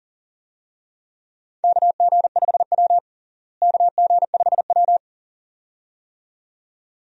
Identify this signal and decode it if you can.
Morse code, heard as a keyed tone, about 40 wpm, 700 Hz. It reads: KG5W KG5W
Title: USA Call Signs - 4 Characters - Repeated 2x - 40wpm